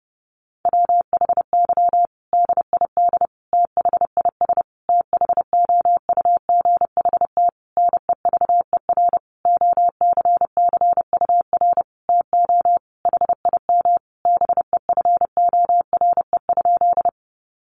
Morse code, heard as a keyed tone, about 29 wpm, 700 Hz. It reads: WHY DID THIS THOUGHT NEVER OCCUR TO HIM BEFORE?
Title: Sentences from Top 900 Words 30wpm